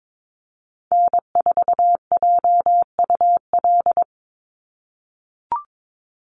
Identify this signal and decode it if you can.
Morse code, heard as a keyed tone, about 22 wpm, 700 Hz. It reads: N4JUL
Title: USA Call Signs 22wpm